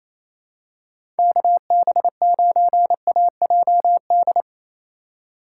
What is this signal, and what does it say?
Morse code, heard as a keyed tone, about 28 wpm, 690 Hz. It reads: KB9AJD
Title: USA Call Signs - 6 Characters - 2x Character Spacing - 28wpm